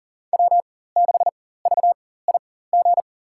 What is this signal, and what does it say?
Morse code, heard as a keyed tone, about 40 wpm, spaced as for 19 wpm, 700 Hz. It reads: W6VIG
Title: USA Call Signs - 5 Characters - 4x Character Spacing - 40wpm